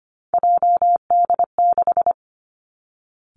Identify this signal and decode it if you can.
Morse code, heard as a keyed tone, about 25 wpm, 700 Hz. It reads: JD6
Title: Three Challenging Letter-Number Combinations 25wpm